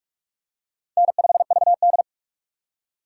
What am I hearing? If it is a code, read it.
Morse code, heard as a keyed tone, about 45 wpm, 700 Hz. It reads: N5VD
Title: USA Call Signs - 4x Character Spacing - 45wpm